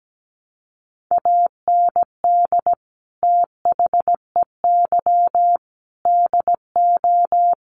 Morse code, heard as a keyed tone, about 17 wpm, 700 Hz. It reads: AND THEY DO
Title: Sets of 5 Words Encore2 17wpm